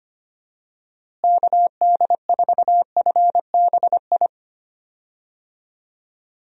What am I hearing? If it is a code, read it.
Morse code, heard as a keyed tone, about 25 wpm, 700 Hz. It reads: KD4FBI
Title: USA Call Signs - 6 Characters - Repeated 3x - 25wpm